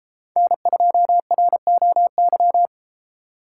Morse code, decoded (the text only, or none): N2ROY